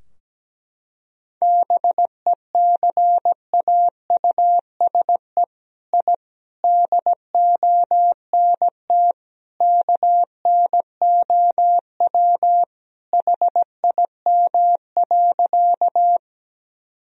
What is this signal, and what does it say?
Morse code, heard as a keyed tone, about 17 wpm, 700 Hz. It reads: BECAUSE I DONT KNOW HIM.
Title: Sentences from Top 1000 Words - Review Part 3 - 17wpm